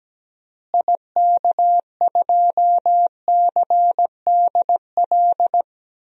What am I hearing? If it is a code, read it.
Morse code, heard as a keyed tone, about 17 wpm, 700 Hz. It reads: IK2CDL
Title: DXCC-100 Call Signs - 6 Characters - Repeated 3x - 17wpm